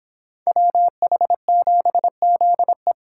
Morse code, heard as a keyed tone, about 25 wpm, 700 Hz. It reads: WH7ZE